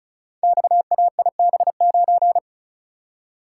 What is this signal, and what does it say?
Morse code, heard as a keyed tone, about 36 wpm, 700 Hz. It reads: XAIB9
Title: Code Groups - 5 Letter:Numbers - 35wpm